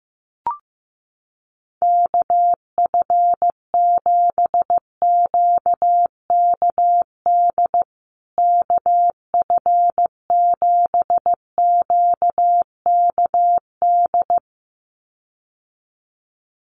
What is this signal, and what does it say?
Morse code, heard as a keyed tone, about 15 wpm, 700 Hz. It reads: KF7QKD KF7QKD
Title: USA Call Signs Repeated 2x 15wpm